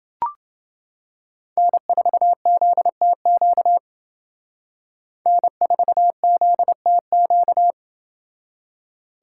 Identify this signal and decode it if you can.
Morse code, heard as a keyed tone, about 28 wpm, 700 Hz. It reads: N4ZTQ N4ZTQ